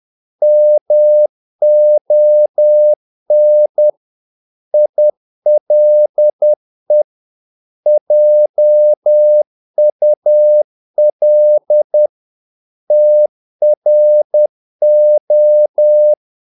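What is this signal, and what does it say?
Morse code, heard as a keyed tone, about 10 wpm, 600 Hz. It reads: MON ILE JUL TRO
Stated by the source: Morse practice of short Danish words at 10 wpm, from CW med Gnister.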